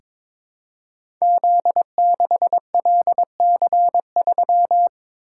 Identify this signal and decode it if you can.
Morse code, heard as a keyed tone, about 22 wpm, 700 Hz. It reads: Z6LC3
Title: Code Groups - 5 Letter:Numbers - 22wpm